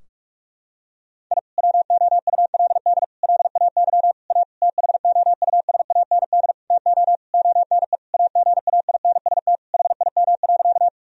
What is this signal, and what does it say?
Morse code, heard as a keyed tone, about 41 wpm, 700 Hz. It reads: I WOULD LAY A THOUSAND TO ONE AGAINST HIM.
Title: Sentences from Top 1000 Words - Review Part 3 - 45wpm